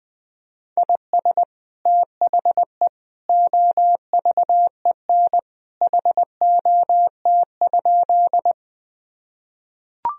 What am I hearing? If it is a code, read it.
Morse code, heard as a keyed tone, about 20 wpm, 700 Hz. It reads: IS THE OVEN HOT?
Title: Sentences from Top 1000 Words - Review Part 4 - 20wpm